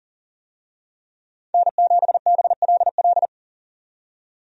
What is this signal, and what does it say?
Morse code, heard as a keyed tone, about 40 wpm, 700 Hz. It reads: N7BLL
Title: USA Call Signs - 5 Characters - 40wpm